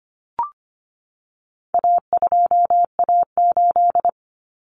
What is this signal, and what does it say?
Morse code, heard as a keyed tone, about 25 wpm, 700 Hz. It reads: A2A8
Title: Common Cut Numbers - 4-Digit 25wpm